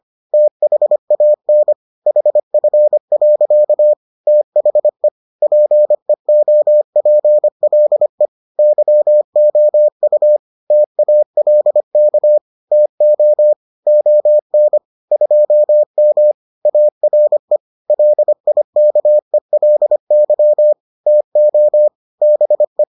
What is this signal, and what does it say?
Morse code, heard as a keyed tone, about 25 wpm, 600 Hz. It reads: THAN HF. THE PEOPLE YOU TALK TO ON 2M ARE LIKELY TO BE